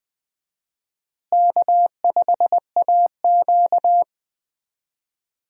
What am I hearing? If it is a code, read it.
Morse code, heard as a keyed tone, about 20 wpm, 700 Hz. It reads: K5AQ
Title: USA Call Signs - 4 Characters - Repeated 2x - 20wpm